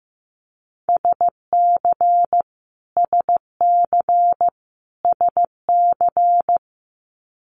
Morse code, heard as a keyed tone, about 15 wpm, 700 Hz. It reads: SC SC SC
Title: Multi-letter Phonemes Repeated 3x 15wpm